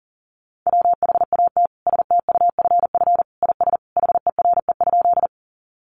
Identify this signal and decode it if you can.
Morse code, heard as a keyed tone, about 40 wpm, 700 Hz. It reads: WHAT STUFF IS HERE?
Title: Sentences from Top 1400 Words 40wpm